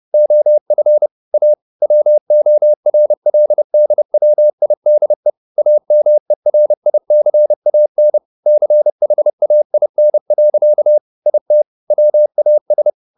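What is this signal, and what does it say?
Morse code, heard as a keyed tone, about 30 wpm, 600 Hz. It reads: OF A WORLDWIDE AMERICAN CHAIN. IT WAS
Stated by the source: Morse code at 30 wpm, the novel Never by Ken Follett in English, from CW med Gnister.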